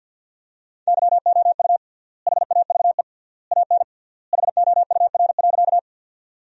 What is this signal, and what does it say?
Morse code, heard as a keyed tone, about 51 wpm, 700 Hz. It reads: YOU HAVE AN HOUR.
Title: 3x Word-Spacing - Sentences from Top 400 Words - 50wpm